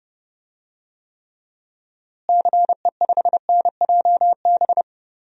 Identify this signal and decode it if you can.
Morse code, heard as a keyed tone, about 32 wpm, 700 Hz. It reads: CE5NJB
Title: DXCC-100 Call Signs - 6 Characters - Repeated 3x - 30wpm